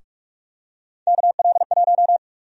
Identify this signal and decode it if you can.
Morse code, heard as a keyed tone, about 46 wpm, 700 Hz. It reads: KL1
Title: USA Call Sign Prefixes - 45wpm